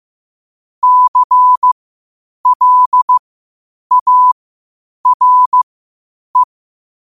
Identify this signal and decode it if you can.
Morse code, heard as a keyed tone, about 15 wpm, spaced as for 8 wpm, 1000 Hz. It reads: CLARE
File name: qotdfiles / 8wpm